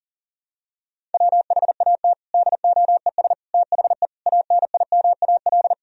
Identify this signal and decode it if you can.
Morse code, heard as a keyed tone, about 40 wpm, 700 Hz. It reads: WHAT DOES THE ANIMAL
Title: Sentences from Top 1000 Words - Review Part 1 - 40wpm